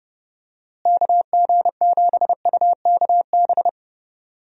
Morse code, heard as a keyed tone, about 30 wpm, 700 Hz. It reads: KG7UKB
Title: USA Call Signs - 6 Characters - Repeated 3x - 30wpm